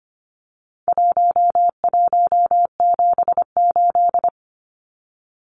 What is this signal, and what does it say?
Morse code, heard as a keyed tone, about 25 wpm, 700 Hz. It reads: 1178
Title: Numbers - 4-Digit 25wpm